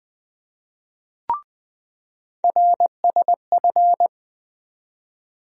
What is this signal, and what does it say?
Morse code, heard as a keyed tone, about 20 wpm, 700 Hz. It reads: RSF
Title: USA Call Sign Postfixes - 3 Letters 20wpm